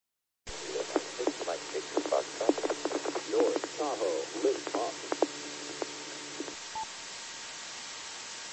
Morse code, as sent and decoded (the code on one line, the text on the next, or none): none